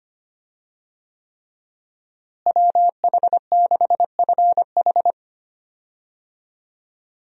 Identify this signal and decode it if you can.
Morse code, heard as a keyed tone, about 25 wpm, 700 Hz. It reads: WH6FH
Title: USA Call Signs - 5 Characters - Repeated 3x - 25wpm